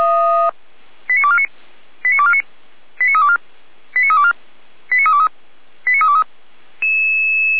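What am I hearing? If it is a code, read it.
Five-tone selective call (ZVEI), heard as quick runs of five tones, tones of about 70 ms. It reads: 89249 89249 89324 89324 89323 89323
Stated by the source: ZVEI five-tone call received on SDR, archived by Feuerwehr Orlamünde.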